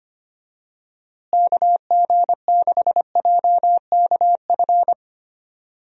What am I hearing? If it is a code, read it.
Morse code, heard as a keyed tone, about 25 wpm, 700 Hz. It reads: KG6JKF
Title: USA Call Signs - 6 Characters - Repeated 2x - 25wpm